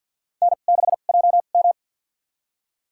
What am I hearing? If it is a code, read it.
Morse code, heard as a keyed tone, about 51 wpm, spaced as for 36 wpm, 700 Hz. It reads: N6JM
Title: USA Call Signs - 4 Characters - 2x Character Spacing - 50wpm